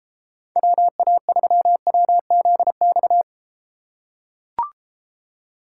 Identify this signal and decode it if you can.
Morse code, heard as a keyed tone, about 34 wpm, 700 Hz. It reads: WA3WZX